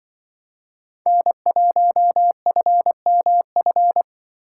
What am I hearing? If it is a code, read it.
Morse code, heard as a keyed tone, about 24 wpm, 700 Hz. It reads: N1FMF
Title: Speed Racing - USA Call Signs 20wpm